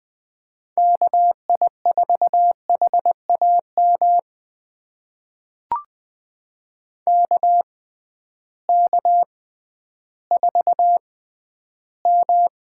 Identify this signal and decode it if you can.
Morse code, heard as a keyed tone, about 20 wpm, spaced as for 12 wpm, 700 Hz. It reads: KI4HAM KK4M
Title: USA Call Signs - 6 Characters - 6x Character Spacing - 20wpm